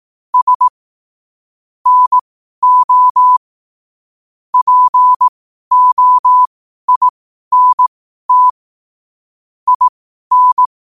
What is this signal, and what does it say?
Morse code, heard as a keyed tone, about 18 wpm, spaced as for 12 wpm, 1000 Hz. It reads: S NO POINT IN
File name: qotdfiles / 12wpm